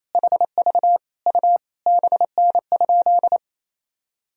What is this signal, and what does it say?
Morse code, heard as a keyed tone, about 28 wpm, 700 Hz. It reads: HV U BN?
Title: Sentences from Top 400 Words and Abbreviations 28wpm